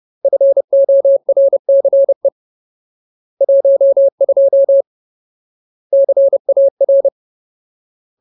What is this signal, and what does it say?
Morse code, heard as a keyed tone, about 30 wpm, 550 Hz. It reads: FORCE 12 CAR
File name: Ultimate Word List / 30WPM Ultimate